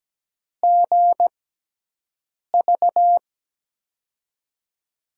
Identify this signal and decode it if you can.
Morse code, heard as a keyed tone, about 17 wpm, spaced as for 6 wpm, 700 Hz. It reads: GV
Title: USA Call Signs - 6 Characters - 6x Character Spacing - 17wpm